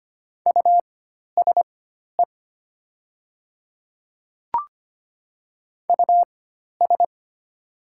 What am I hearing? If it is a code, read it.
Morse code, heard as a keyed tone, about 25 wpm, spaced as for 12 wpm, 700 Hz. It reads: USE US
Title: 4x Character Spacing - Top 300 Words - 25wpm